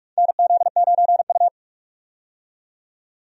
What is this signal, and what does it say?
Morse code, heard as a keyed tone, about 41 wpm, 700 Hz. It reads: NZ9U
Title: USA Call Signs - 4 Characters - 45wpm